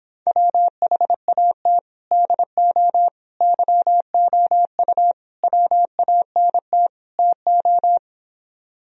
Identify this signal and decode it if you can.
Morse code, heard as a keyed tone, about 26 wpm, 700 Hz. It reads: WHAT DO YOU WANT TO